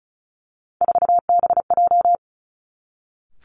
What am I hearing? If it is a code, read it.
Morse code, heard as a keyed tone, about 36 wpm, 700 Hz. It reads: 4BJ
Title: Three Challenging Letter-Number Combinations 35wpm